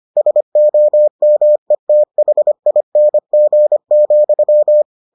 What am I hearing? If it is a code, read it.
Morse code, heard as a keyed tone, about 25 wpm, 600 Hz. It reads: SOMETHING,